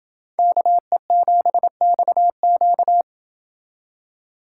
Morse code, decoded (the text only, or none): KE7XQ